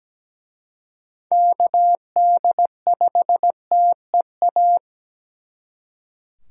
Morse code, heard as a keyed tone, about 20 wpm, 700 Hz. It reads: KD5TEA
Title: Speed Racing - USA 2x3 Call Signs 17wpm